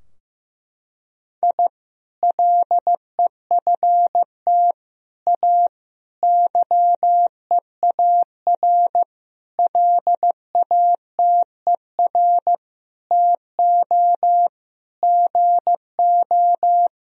Morse code, heard as a keyed tone, about 15 wpm, 700 Hz. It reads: I LEFT A YEAR LATER TO GO
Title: Sentences from Top 1000 Words - Review Part 4 - 15wpm